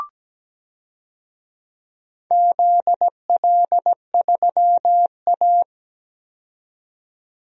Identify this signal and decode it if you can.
Morse code, heard as a keyed tone, about 17 wpm, 700 Hz. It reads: ZL3A
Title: DXCC-100 Call Signs - 4 Characters - 17wpm